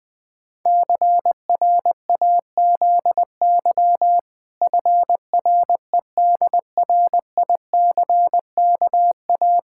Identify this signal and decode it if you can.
Morse code, heard as a keyed tone, about 21 wpm, 700 Hz. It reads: CRAZY FREDRICKA
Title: Pangram Challenge v2 - 20wpm